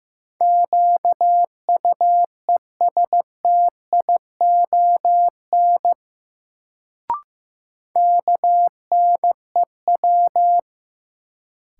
Morse code, heard as a keyed tone, about 15 wpm, 700 Hz. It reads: QUESTION KNEW
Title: Sentences from Top 400 Words - New Word Drill - 15wpm